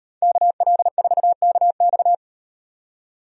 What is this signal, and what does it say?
Morse code, heard as a keyed tone, about 38 wpm, 700 Hz. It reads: KL4KX